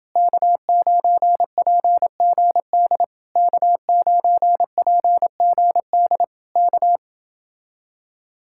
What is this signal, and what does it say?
Morse code, heard as a keyed tone, about 28 wpm, 700 Hz. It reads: K9PGD K9PGD K